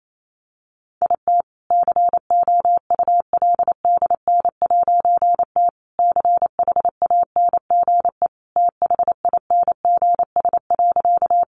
Morse code, heard as a keyed tone, about 28 wpm, 700 Hz. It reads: IT COULDN'T CHANGE THINGS.
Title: A Sound Of Thunder 28wpm